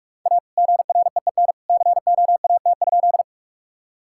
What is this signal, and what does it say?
Morse code, heard as a keyed tone, about 46 wpm, 700 Hz. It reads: A GREEN COAT?
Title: Sentences from Top 1600 Words 45wpm